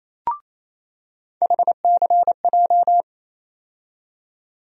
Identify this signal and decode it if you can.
Morse code, heard as a keyed tone, about 28 wpm, 700 Hz. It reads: HCJ
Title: USA Call Sign Postfixes - 3 Letters 28wpm